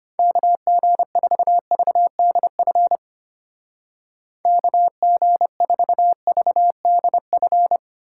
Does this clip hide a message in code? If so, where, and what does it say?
Yes: Morse code from 0.2 to 7.8 s, heard as a keyed tone, about 32 wpm, 700 Hz. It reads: KG4VDF KG4VDF